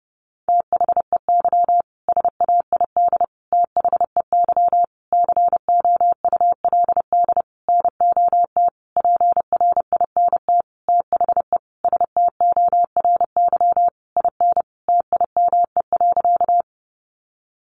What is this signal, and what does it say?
Morse code, heard as a keyed tone, about 30 wpm, 700 Hz. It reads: THEY SAID THEY COULD NOT PRINT THE STORY IN TIME.
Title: Sentences from Top 1400 Words 30wpm